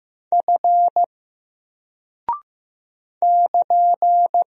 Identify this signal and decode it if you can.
Morse code, heard as a keyed tone, about 15 wpm, 700 Hz. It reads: F <KN>
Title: Single Character (Letter:Number:Punctuation:Prosign) Alt 15wpm